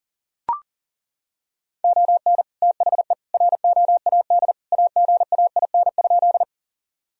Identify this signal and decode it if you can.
Morse code, heard as a keyed tone, about 37 wpm, 700 Hz. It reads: ON THE ROAD AGAIN?
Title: Sentences from Top 1000 Words - Review Part 2 - 40wpm